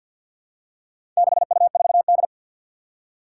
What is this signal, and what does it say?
Morse code, heard as a keyed tone, about 52 wpm, 700 Hz. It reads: 6U4D
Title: Four Challenging Letter-Number Combinations 50wpm